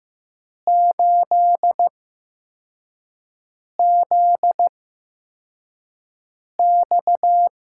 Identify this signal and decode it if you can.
Morse code, heard as a keyed tone, about 15 wpm, spaced as for 4 wpm, 700 Hz. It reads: 8ZX
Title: USA Call Signs - 6 Characters - 8x Character Spacing - 15wpm